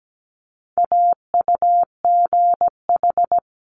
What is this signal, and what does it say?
Morse code, heard as a keyed tone, about 17 wpm, 700 Hz. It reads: AUGH
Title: Multi-letter Phonemes Quick Fire 17wpm